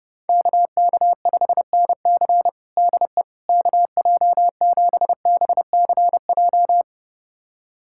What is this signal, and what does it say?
Morse code, heard as a keyed tone, about 30 wpm, 700 Hz. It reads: KK5NC DE KJ7BCJ
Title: USA Callsign DE Callsign 30wpm